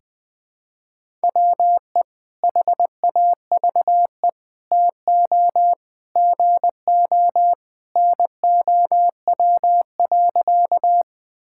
Morse code, heard as a keyed tone, about 20 wpm, 700 Hz. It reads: WE HAVE TO GO NOW.
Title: Sentences with 3-Word Phrases - 20wpm